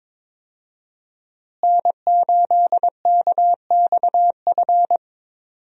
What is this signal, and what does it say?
Morse code, heard as a keyed tone, about 22 wpm, 700 Hz. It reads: N8KXF
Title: USA Call Signs - 5 Characters - Repeated 3x - 22wpm